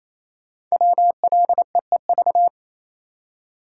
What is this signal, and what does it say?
Morse code, heard as a keyed tone, about 27 wpm, 700 Hz. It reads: WLEEV
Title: Code Groups - 5 Letters - 28wpm